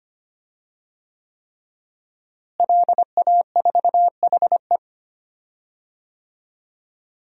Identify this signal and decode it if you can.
Morse code, heard as a keyed tone, about 25 wpm, 700 Hz. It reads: LA4HE
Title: DXCC-100 Call Signs - 5 Characters - Repeated 3x - 25wpm